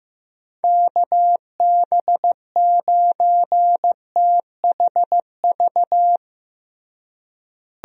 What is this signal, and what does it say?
Morse code, heard as a keyed tone, about 15 wpm, 700 Hz. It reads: KB9THV
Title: USA Call Signs - 6 Characters - Repeated 2x - 15wpm